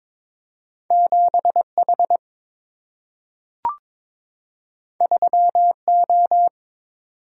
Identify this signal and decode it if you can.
Morse code, heard as a keyed tone, about 22 wpm, 700 Hz. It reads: 7 H 3 O
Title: Single-Letter Single-Number 22wpm